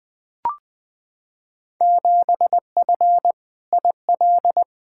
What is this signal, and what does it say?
Morse code, heard as a keyed tone, about 20 wpm, 700 Hz. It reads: 7F IL
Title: Field Day Exchanges Encore - 20wpm